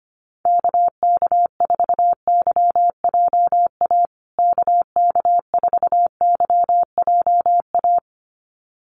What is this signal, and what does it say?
Morse code, heard as a keyed tone, about 25 wpm, 700 Hz. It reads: KK4YJA KK4YJA
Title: USA Call Signs Repeated 2x 25wpm